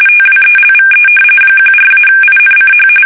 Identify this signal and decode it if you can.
RTTY, about 45.45 baud. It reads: TWELVE 14-INCH
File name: fsk850 / battleshipnj